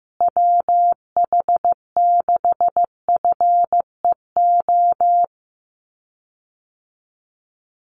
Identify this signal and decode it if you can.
Morse code, heard as a keyed tone, about 14 wpm, 700 Hz. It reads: WH6FEO
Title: USA Call Signs Repeated 3x 15wpm